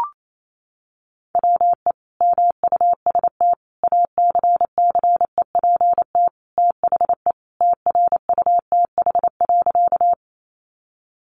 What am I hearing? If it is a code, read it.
Morse code, heard as a keyed tone, about 28 wpm, 700 Hz. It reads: WE MUST ACCEPT THE TRUTH.
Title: Sentences from Top 1300 Words 28wpm